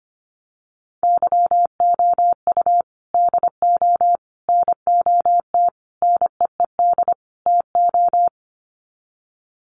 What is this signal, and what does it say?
Morse code, heard as a keyed tone, about 25 wpm, 700 Hz. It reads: YOU DO NOT NEED TO
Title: Sets of 5 Words 25wpm